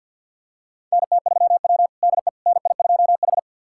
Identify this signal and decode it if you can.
Morse code, heard as a keyed tone, about 45 wpm, 700 Hz. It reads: NT3W DE NI2H
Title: USA Callsign DE Callsign - 4 Characters - 50wpm